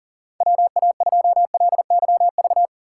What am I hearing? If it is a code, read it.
Morse code, heard as a keyed tone, about 40 wpm, 700 Hz. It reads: WA2LYV
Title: USA Call Signs - 6 Characters - 3x Character Spacing - 40wpm